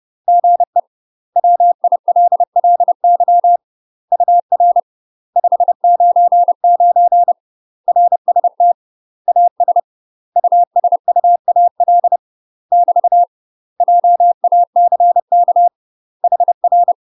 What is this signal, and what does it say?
Morse code, heard as a keyed tone, about 30 wpm, 700 Hz. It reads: GE WILLY UR 599 RST AS USUAL <BT> JACK HR
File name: UK MASTER QSOs TRAINING / 30WPM / UK MASTER QSOs 0008